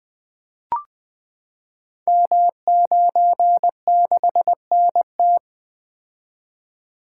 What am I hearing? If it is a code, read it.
Morse code, heard as a keyed tone, about 20 wpm, 700 Hz. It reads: M96NT
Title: Code Groups - 5 Letter:Numbers - 20wpm